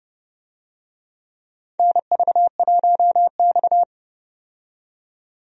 Morse code, heard as a keyed tone, about 30 wpm, 700 Hz. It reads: NV1X
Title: USA Call Signs - 4 Characters - 30wpm